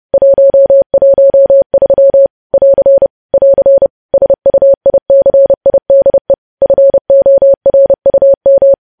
Morse code, heard as a keyed tone, about 30 wpm, 550 Hz. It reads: 113 <AR> <AR> SUICIDE FORUM
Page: news headlines in morse code at 30 wpm